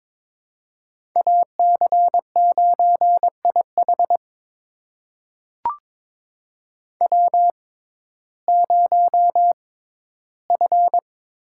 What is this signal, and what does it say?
Morse code, heard as a keyed tone, about 22 wpm, spaced as for 14 wpm, 700 Hz. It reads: AC9IH W0F
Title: USA Call Signs - 5 Characters - 6x Character Spacing - 22wpm